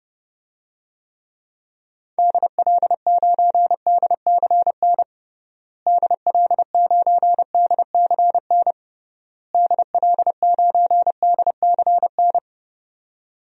Morse code, heard as a keyed tone, about 30 wpm, 700 Hz. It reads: DL9DCN DL9DCN DL9DCN
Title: DXCC-100 Call Signs - 6 Characters - Repeated 3x - 30wpm